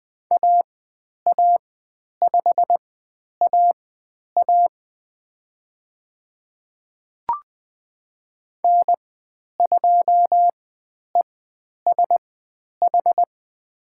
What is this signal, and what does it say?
Morse code, heard as a keyed tone, about 20 wpm, spaced as for 10 wpm, 700 Hz. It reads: AA5AA N2ESH
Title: 102 Lesson 17g - Letter u - Call Signs 20wpm